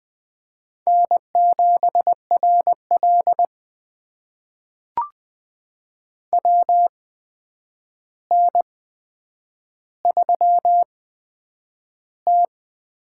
Morse code, heard as a keyed tone, about 20 wpm, spaced as for 9 wpm, 700 Hz. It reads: N7RL WN3T
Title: USA Call Signs - 4 Characters - 8x Character Spacing - 20wpm